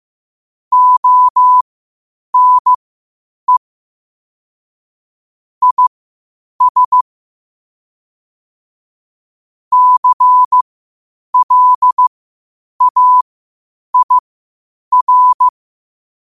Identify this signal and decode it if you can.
Morse code, heard as a keyed tone, about 15 wpm, spaced as for 8 wpm, 1000 Hz. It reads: ONE IS CLAIR
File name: qotdfiles / 8wpm